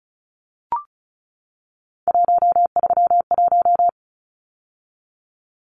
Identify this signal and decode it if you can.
Morse code, heard as a keyed tone, about 34 wpm, 700 Hz. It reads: 131
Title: Numbers - 3-Digit 35wpm